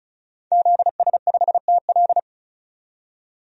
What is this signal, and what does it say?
Morse code, heard as a keyed tone, about 36 wpm, 700 Hz. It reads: ZS5TL
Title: DXCC-100 Call Signs - 5 Characters - 35wpm